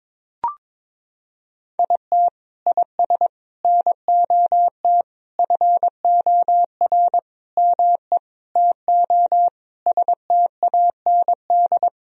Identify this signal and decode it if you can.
Morse code, heard as a keyed tone, about 22 wpm, 700 Hz. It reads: IT IS NOT FOR ME TO STAND
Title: Sentences from Top 1000 Words - Review Part 1 - 22wpm